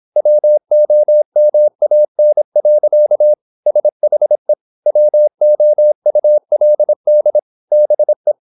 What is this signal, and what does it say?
Morse code, heard as a keyed tone, about 26 wpm, 600 Hz. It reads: WOMAN. SHE WOULD BE
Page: Never 26wpm | CW med Gnister